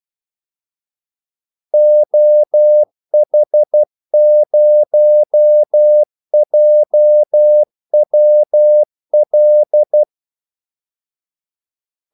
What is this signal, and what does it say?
Morse code, heard as a keyed tone, about 12 wpm, 600 Hz. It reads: OH0JWL
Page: 2 ekstra ordmellemrum mellem kaldesignalerne